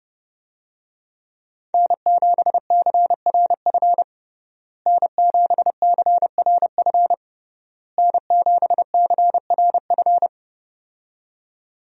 Morse code, heard as a keyed tone, about 30 wpm, 700 Hz. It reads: N7CRF N7CRF N7CRF
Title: USA Call Signs - 5 Characters - Repeated 3x - 30wpm